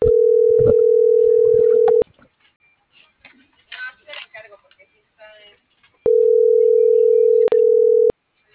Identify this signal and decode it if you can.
Ringback tone (North America), 440 + 480 Hz.